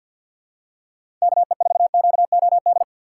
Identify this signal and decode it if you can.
Morse code, heard as a keyed tone, about 47 wpm, 700 Hz. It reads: KE4QOD
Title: USA Call Signs - 3x Character Spacing - 50wpm